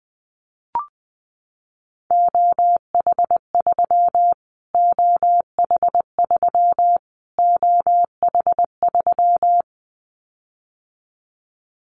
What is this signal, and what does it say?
Morse code, heard as a keyed tone, about 20 wpm, 700 Hz. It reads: OH3 OH3 OH3
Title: Callsign Prefixes for Top 25 DXCC - Repeated 3x - 20wpm